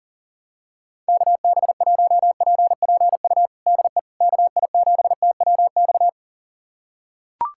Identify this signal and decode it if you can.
Morse code, heard as a keyed tone, about 41 wpm, 690 Hz. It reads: KB1PPU DE KI7TWX
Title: USA Callsign DE Callsign - 6 Characters - 40wpm